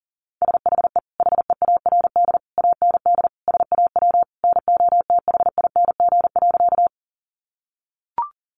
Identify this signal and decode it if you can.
Morse code, heard as a keyed tone, about 37 wpm, 700 Hz. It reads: SHE HEARD AND SAW NOTHING.
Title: Sentences from Top 500 Words 40wpm